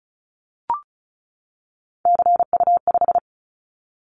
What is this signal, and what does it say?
Morse code, heard as a keyed tone, about 36 wpm, 700 Hz. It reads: CU5
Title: Callsign Prefixes for Top 100 DXCC - 35wpm